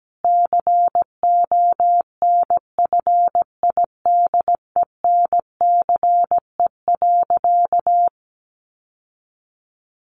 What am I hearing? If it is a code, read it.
Morse code, heard as a keyed tone, about 17 wpm, 700 Hz. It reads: CONFIDENCE.
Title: Sentences from Top 1500 Words 17wpm